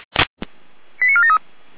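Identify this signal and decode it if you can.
Five-tone selective call (ZVEI), heard as quick runs of five tones, tones of about 70 ms. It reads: 89473
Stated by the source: ZVEI five-tone call 89457 received on SDR.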